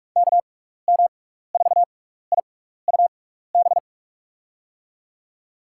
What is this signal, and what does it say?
Morse code, heard as a keyed tone, about 46 wpm, spaced as for 16 wpm, 700 Hz. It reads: KM4IUB
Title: USA Call Signs - 6x Character Spacing - 45wpm